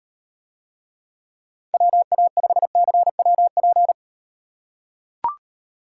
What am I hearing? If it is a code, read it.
Morse code, heard as a keyed tone, about 38 wpm, 700 Hz. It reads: WA5CWP